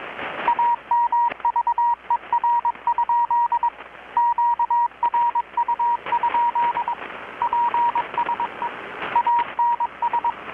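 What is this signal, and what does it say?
Morse code, heard as a keyed tone, about 22 wpm, 950 Hz, with radio noise. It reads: AMVER? QRU? PSE ANS
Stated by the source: Morse from KPH's Night of Nights XX maritime radiotelegraphy event, received in Chiapas, Mexico about 3,800 km away on a Kenwood R-600 with a grounded 10 m long-wire antenna.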